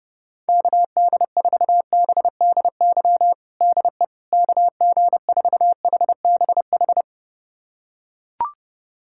Morse code, heard as a keyed tone, about 30 wpm, 700 Hz. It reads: KD4BDY DE KG4HBH
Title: USA Callsign DE Callsign 30wpm